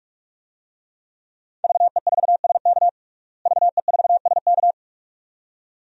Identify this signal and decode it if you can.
Morse code, heard as a keyed tone, about 46 wpm, 700 Hz. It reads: VE4SK VE4SK
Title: DXCC-100 Call Signs - 5 Characters - Repeated 2x - 45wpm